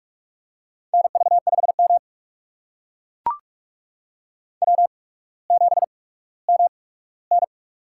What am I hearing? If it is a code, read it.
Morse code, heard as a keyed tone, about 44 wpm, spaced as for 19 wpm, 700 Hz. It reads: NV5M W7MN
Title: USA Call Signs - 4 Characters - 8x Character Spacing - 45wpm